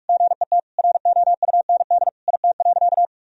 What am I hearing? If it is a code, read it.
Morse code, heard as a keyed tone, about 46 wpm, 700 Hz. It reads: GET ROUND IT.
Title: Sentences from Top 1000 Words - Review Part 1 - 45wpm